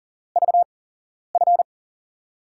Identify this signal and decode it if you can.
Morse code, heard as a keyed tone, about 44 wpm, spaced as for 11 wpm, 700 Hz. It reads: VF
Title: USA Call Signs - 8x Character Spacing - 40wpm